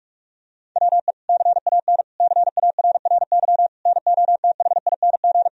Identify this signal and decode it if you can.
Morse code, heard as a keyed tone, about 41 wpm, 700 Hz. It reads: WE CAN CARRY NOTHING
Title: Sentences from Top 1000 Words - Review Part 1 - 45wpm